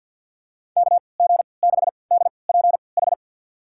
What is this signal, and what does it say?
Morse code, heard as a keyed tone, about 46 wpm, spaced as for 27 wpm, 700 Hz. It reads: KG6DPH
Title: USA Call Signs - 3x Character Spacing - 50wpm